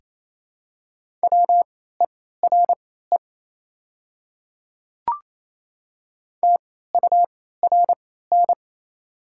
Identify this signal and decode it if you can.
Morse code, heard as a keyed tone, about 27 wpm, spaced as for 16 wpm, 690 Hz. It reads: WERE TURN
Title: 3x Character Spacing - Top 300 Words - 28wpm